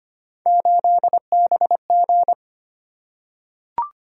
Morse code, heard as a keyed tone, about 25 wpm, 700 Hz. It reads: 8BG
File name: Three Letter-Number Combinations - 25wpm